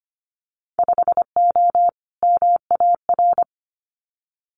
Morse code, heard as a keyed tone, about 25 wpm, 700 Hz. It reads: 5O MAR
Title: Winter Field Day Exchanges Encore 25wpm